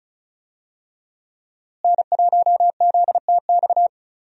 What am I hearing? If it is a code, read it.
Morse code, heard as a keyed tone, about 34 wpm, 700 Hz. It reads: N1ZTX
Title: USA Call Signs - 5 Characters - Repeated 3x - 35wpm